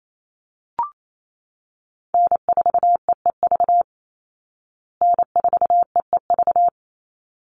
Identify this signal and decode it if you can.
Morse code, heard as a keyed tone, about 28 wpm, 700 Hz. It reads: N4EEV N4EEV
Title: Call Signs Repeated 3x - 28wpm